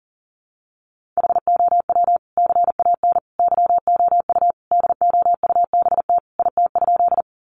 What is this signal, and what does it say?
Morse code, heard as a keyed tone, about 40 wpm, 700 Hz. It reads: HOW CAN YOU DOUBT IT?
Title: Sentences from Top 1500 Words 40wpm